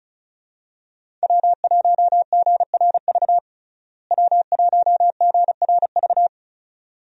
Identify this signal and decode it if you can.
Morse code, heard as a keyed tone, about 35 wpm, 700 Hz. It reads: W1GRV W1GRV
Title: USA Call Signs - 5 Characters - Repeated 2x - 35wpm